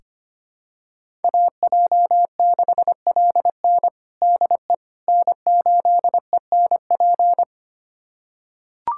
Morse code, heard as a keyed tone, about 25 wpm, 700 Hz. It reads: AJ6LN DE N8ENP
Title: USA Callsign DE Callsign - 5 Characters - 25wpm